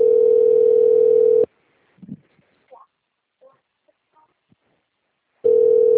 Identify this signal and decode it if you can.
Ringback tone (North America), 440 + 480 Hz.
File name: out- - - - - .21795